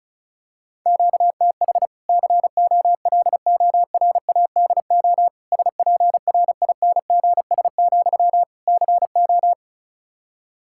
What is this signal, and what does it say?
Morse code, heard as a keyed tone, about 34 wpm, 700 Hz. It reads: QTH COLORADO SPRINGS, CO
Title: Practice QSO - Exchange 1 - Beginning - 35wpm